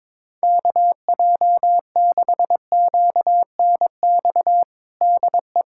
Morse code, heard as a keyed tone, about 22 wpm, 700 Hz. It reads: KJ6QNX DE
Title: USA Callsign DE Callsign - 6 Characters - 22wpm